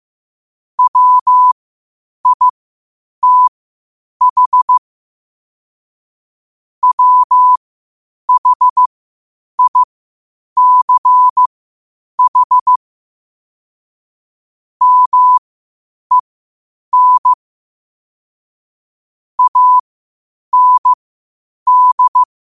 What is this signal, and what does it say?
Morse code, heard as a keyed tone, about 15 wpm, spaced as for 8 wpm, 1000 Hz. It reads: WITH WHICH MEN AND
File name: qotdfiles / 8wpm